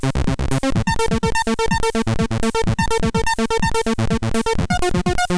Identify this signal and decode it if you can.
There is no code.